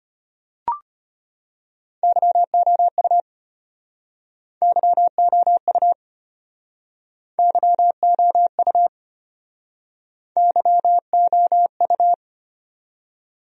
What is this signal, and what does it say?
Morse code, heard as a keyed tone, about 34 wpm, 700 Hz. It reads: YOU YOU YOU YOU